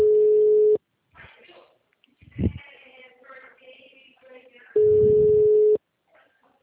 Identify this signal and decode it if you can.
Ringback tone (Europe, 425 Hz), with speech.
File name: out- -2020- - - .54188